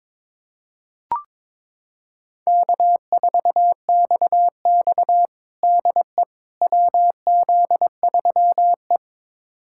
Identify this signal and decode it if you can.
Morse code, heard as a keyed tone, about 22 wpm, 700 Hz. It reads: K4XX DE WZ3E
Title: USA Callsign DE Callsign - 4 Characters - 22wpm